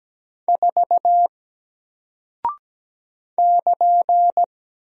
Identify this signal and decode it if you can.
Morse code, heard as a keyed tone, about 17 wpm, 700 Hz. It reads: 4 <KN>
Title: Single Character (Letter:Number:Punctuation:Prosign) Alt 17wpm